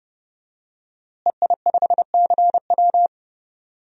Morse code, heard as a keyed tone, about 30 wpm, 700 Hz. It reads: EI5CW
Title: DXCC-100 Call Signs - 5 Characters - Repeated 2x - 30wpm